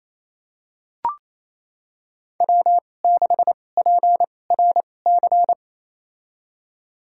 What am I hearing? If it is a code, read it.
Morse code, heard as a keyed tone, about 28 wpm, spaced as for 20 wpm, 700 Hz. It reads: W6PRC
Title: USA Call Signs - 2x Character Spacing - 28wpm